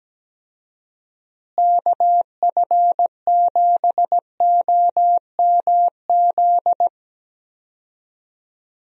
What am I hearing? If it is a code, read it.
Morse code, heard as a keyed tone, about 17 wpm, 700 Hz. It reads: KF7OMZ
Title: USA Call Signs - 6 Characters - Repeated 2x - 17wpm